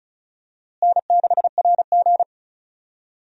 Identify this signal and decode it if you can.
Morse code, heard as a keyed tone, about 36 wpm, 700 Hz. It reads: N6RG
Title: USA Call Signs - 4 Characters - Repeated 2x - 35wpm